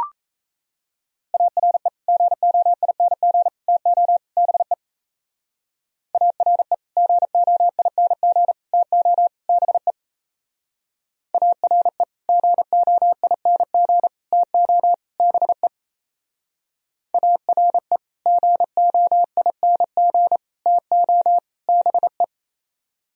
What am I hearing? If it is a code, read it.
Morse code, heard as a keyed tone, about 36 wpm, 700 Hz. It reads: ARE GOING TO BE ARE GOING TO BE ARE GOING TO BE ARE GOING TO BE